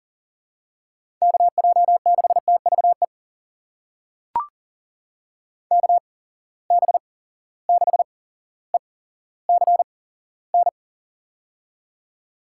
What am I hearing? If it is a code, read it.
Morse code, heard as a keyed tone, about 40 wpm, spaced as for 17 wpm, 700 Hz. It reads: KJ6TVE KB6ECN
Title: USA Call Signs - 6 Characters - 8x Character Spacing - 40wpm